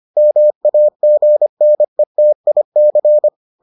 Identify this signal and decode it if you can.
Morse code, heard as a keyed tone, about 25 wpm, 600 Hz. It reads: MAGNETIC